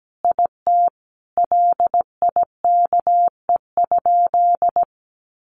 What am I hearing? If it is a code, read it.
Morse code, heard as a keyed tone, about 17 wpm, 700 Hz. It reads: IT LIKE?
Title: Sentences from Top 100 Words 17wpm